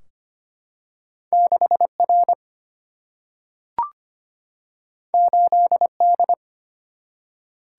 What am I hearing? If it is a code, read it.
Morse code, heard as a keyed tone, about 25 wpm, 700 Hz. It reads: 6R 8D
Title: Single-Letter Single-Number 25wpm